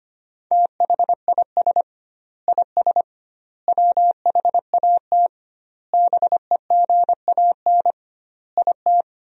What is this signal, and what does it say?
Morse code, heard as a keyed tone, about 25 wpm, 700 Hz. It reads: THIS IS WHAT BEGAN IT
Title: 2x Word-Spacing - Sentences from Top 300 Words - 25wpm